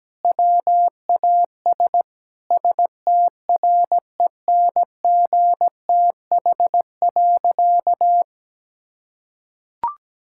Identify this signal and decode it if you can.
Morse code, heard as a keyed tone, about 17 wpm, 700 Hz. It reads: WAS STRENGTH.
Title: Sentences from Top 1600 Words 17wpm